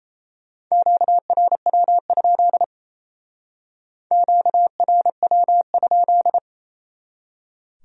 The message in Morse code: QRW? QRW?

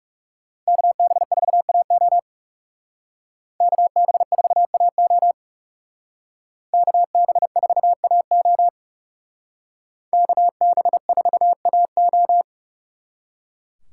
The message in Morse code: KB4AO KB4AO KB4AO KB4AO

